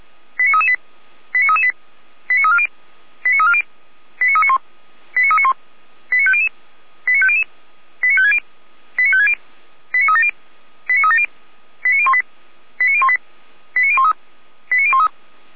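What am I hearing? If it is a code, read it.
Five-tone selective call (ZVEI), heard as quick runs of five tones, tones of about 70 ms. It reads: 89338 89338 89340 89340 89391 89391 89500 89500 89560 89560 89370 89370 89018 89018 89013 89013
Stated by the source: ZVEI five-tone selective call received by SDR, from the feuerwehr-orlamuende fire brigade audio server.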